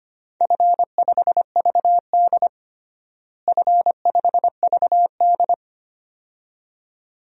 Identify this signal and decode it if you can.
Morse code, heard as a keyed tone, about 25 wpm, 700 Hz. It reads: F5VD F5VD
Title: DXCC-100 Call Signs - 4 Characters - Repeated 2x - 25wpm